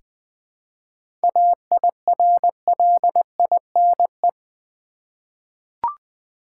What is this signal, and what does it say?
Morse code, heard as a keyed tone, about 20 wpm, 700 Hz. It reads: AIRLINE
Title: Seven Letter Words - 20wpm